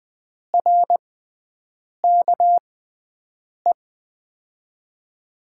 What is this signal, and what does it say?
Morse code, heard as a keyed tone, about 20 wpm, spaced as for 7 wpm, 700 Hz. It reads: RKE
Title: USA Call Signs - 6 Characters - 6x Character Spacing - 20wpm